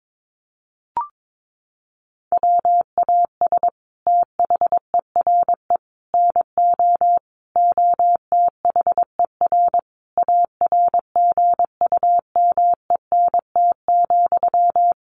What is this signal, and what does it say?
Morse code, heard as a keyed tone, about 22 wpm, 700 Hz. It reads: WAS THERE NO OTHER ARGUMENT,
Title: Sentences from Top 1400 Words 22wpm